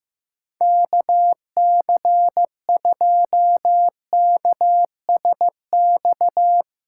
Morse code, heard as a keyed tone, about 15 wpm, 700 Hz. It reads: KC2KSX
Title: USA Call Signs - 6 Characters - 6x Character Spacing - 15wpm